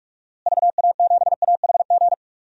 Morse code, heard as a keyed tone, about 46 wpm, 700 Hz. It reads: VA7AHG
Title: DXCC-100 Call Signs - 6 Characters - Repeated 2x - 45wpm